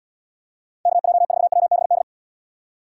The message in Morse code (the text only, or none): KJ4WBD